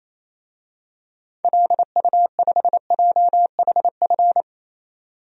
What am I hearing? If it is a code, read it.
Morse code, heard as a keyed tone, about 28 wpm, 700 Hz. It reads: LU5JHF
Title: DXCC-100 Call Signs - 6 Characters - 28wpm